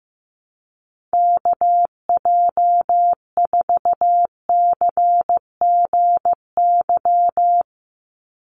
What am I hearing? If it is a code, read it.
Morse code, heard as a keyed tone, about 15 wpm, 700 Hz. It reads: KJ4CGY
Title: USA Call Signs 15wpm